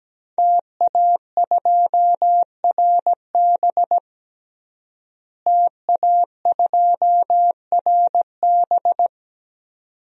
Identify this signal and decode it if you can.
Morse code, heard as a keyed tone, about 17 wpm, 700 Hz. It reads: TA2RB TA2RB
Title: DXCC-100 Call Signs - 5 Characters - Repeated 2x - 17wpm